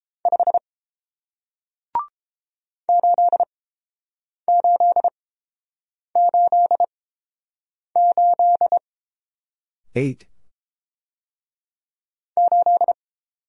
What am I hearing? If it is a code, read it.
Morse code, heard as a keyed tone, about 30 wpm, spaced as for 7 wpm, 700 Hz. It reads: 5 8888 8